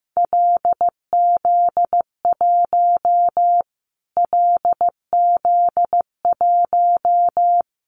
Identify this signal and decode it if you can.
Morse code, heard as a keyed tone, about 15 wpm, 700 Hz. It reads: LZ1 LZ1
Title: Callsign Prefixes for Top 25 DXCC - Repeated 3x - 15wpm